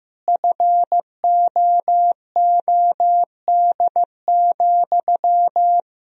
Morse code, heard as a keyed tone, about 15 wpm, 700 Hz. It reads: FOOD,
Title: Sentences from Top 1000 Words - Review Part 1 - 15wpm